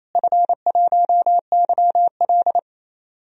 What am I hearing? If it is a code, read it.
Morse code, heard as a keyed tone, about 28 wpm, 700 Hz. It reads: F1YL
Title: DXCC-100 Call Signs - 4 Characters - Repeated 3x - 28wpm